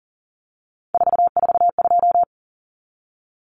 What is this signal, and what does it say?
Morse code, heard as a keyed tone, about 40 wpm, 700 Hz. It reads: 442
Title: Numbers - 3-Digit 40wpm